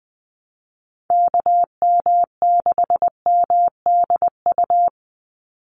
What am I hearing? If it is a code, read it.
Morse code, heard as a keyed tone, about 20 wpm, 700 Hz. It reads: KM6MDU
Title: USA Call Signs 20wpm